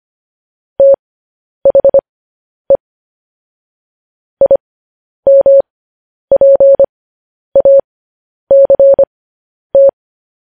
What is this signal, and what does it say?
Morse code, heard as a keyed tone, about 25 wpm, spaced as for 10 wpm, 550 Hz. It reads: THE IMPACT
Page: news-headlines-in-morse-code-at-10-wpm